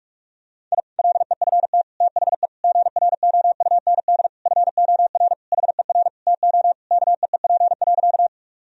Morse code, heard as a keyed tone, about 46 wpm, 700 Hz. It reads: I LEFT THE GROUND FOR HER TO KEEP.
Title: Sentences from Top 1000 Words - Review Part 2 - 45wpm